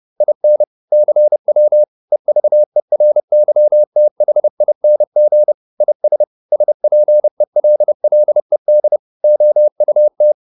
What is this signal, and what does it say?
Morse code, heard as a keyed tone, about 30 wpm, 600 Hz. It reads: IN CW EVERYTHING IS SPELLED OUT